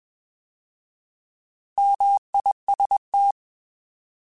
Morse code, heard as a keyed tone, about 21 wpm, 800 Hz. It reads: MIST
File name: Alphabetical / Alphabetical mp3s / Ms @20